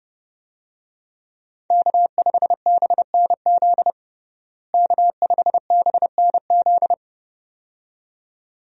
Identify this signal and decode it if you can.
Morse code, heard as a keyed tone, about 30 wpm, 700 Hz. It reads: K5BNZ K5BNZ
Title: USA Call Signs - 5 Characters - Repeated 2x - 30wpm